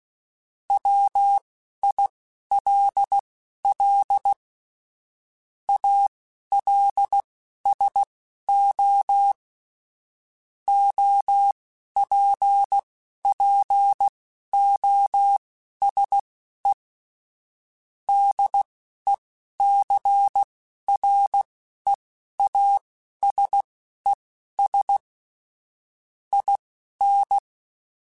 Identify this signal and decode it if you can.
Morse code, heard as a keyed tone, about 16 wpm, spaced as for 11 wpm, 800 Hz. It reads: WILL ALSO OPPOSE DECREASES IN
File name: Technical Subjects / Technical Subjects-mp3s / XC @10